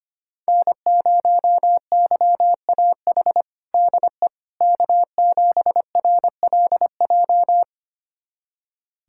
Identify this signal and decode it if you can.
Morse code, heard as a keyed tone, about 25 wpm, 700 Hz. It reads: N0YAH DE K7RLJ
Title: USA Callsign DE Callsign - 5 Characters - 25wpm